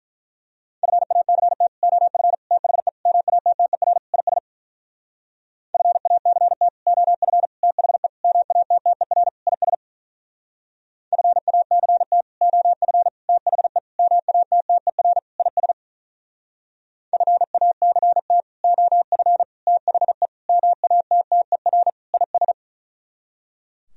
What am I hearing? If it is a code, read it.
Morse code, heard as a keyed tone, about 44 wpm, 700 Hz. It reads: FACT OF THE MATTER IS FACT OF THE MATTER IS FACT OF THE MATTER IS FACT OF THE MATTER IS